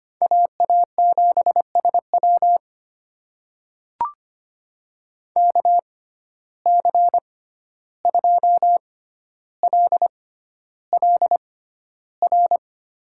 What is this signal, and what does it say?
Morse code, heard as a keyed tone, about 25 wpm, spaced as for 12 wpm, 700 Hz. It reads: AA7SW KC2LLR
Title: USA Call Signs - 6x Character Spacing - 25wpm